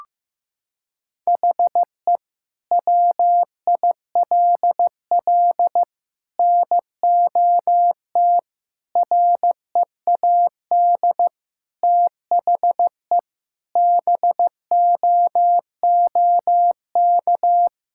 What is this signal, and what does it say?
Morse code, heard as a keyed tone, about 15 wpm, 700 Hz. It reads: HE WILL NOT READ THE BOOK
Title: Sentences from Top 1600 Words 15wpm